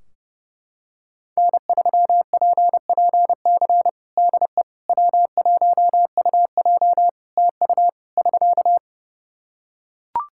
Morse code, heard as a keyed tone, about 30 wpm, 700 Hz. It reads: N3PPC DE W1UJ TU <SK>
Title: Practice QSO - Exchange 4 - Ending - 30wpm